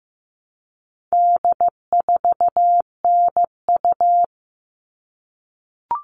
Morse code, heard as a keyed tone, about 15 wpm, 700 Hz. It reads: D4NU
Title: All Cut Numbers - 4-Digit 15wpm